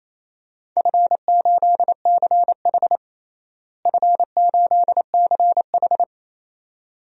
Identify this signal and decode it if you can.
Morse code, heard as a keyed tone, about 28 wpm, 700 Hz. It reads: F8CH F8CH
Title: DXCC-100 Call Signs - 4 Characters - Repeated 2x - 28wpm